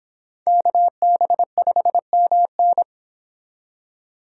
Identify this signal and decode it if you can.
Morse code, heard as a keyed tone, about 26 wpm, 700 Hz. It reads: KB5MN